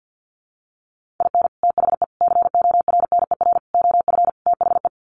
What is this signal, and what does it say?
Morse code, heard as a keyed tone, about 50 wpm, 700 Hz. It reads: IN THE CORNER OF THE
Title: Sets of 5 Words Encore 50wpm